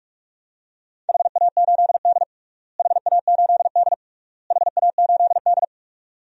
Morse code, heard as a keyed tone, about 41 wpm, 700 Hz. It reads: HA8D HA8D HA8D
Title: DXCC-100 Call Signs - 4 Characters - Repeated 3x - 45wpm